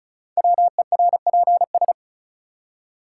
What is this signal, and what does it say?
Morse code, heard as a keyed tone, about 36 wpm, 700 Hz. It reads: WERPS
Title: Code Groups - 5 Letters - 35wpm